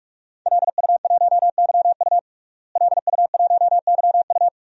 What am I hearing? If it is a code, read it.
Morse code, heard as a keyed tone, about 48 wpm, 700 Hz. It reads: LU1YU LU1YU
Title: DXCC-100 Call Signs - 5 Characters - Repeated 2x - 45wpm